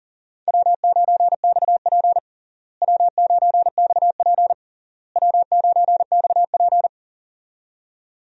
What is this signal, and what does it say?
Morse code, heard as a keyed tone, about 40 wpm, 700 Hz. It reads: W9XP W9XP W9XP
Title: USA Call Signs - 4 Characters - Repeated 3x - 40wpm